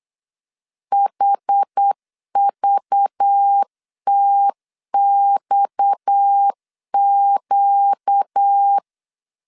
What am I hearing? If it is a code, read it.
Morse code, heard as a keyed tone, about 8 wpm, 800 Hz. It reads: HVTXQ
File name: CW / DET / 1 6 1500C S B 8wpm 5142